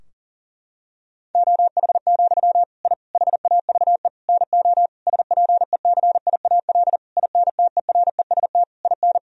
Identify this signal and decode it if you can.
Morse code, heard as a keyed tone, about 40 wpm, 700 Hz. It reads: OH, I HAVE NO SPECIAL INTEREST IN
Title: Sentences from Top 1000 Words - Review Part 3 - 40wpm